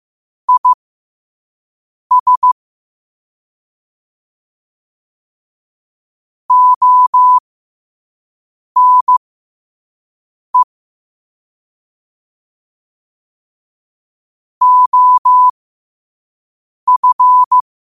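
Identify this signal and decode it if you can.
Morse code, heard as a keyed tone, about 15 wpm, spaced as for 5 wpm, 1000 Hz. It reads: IS ONE OF
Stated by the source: 5 wpm Morse code practice recording.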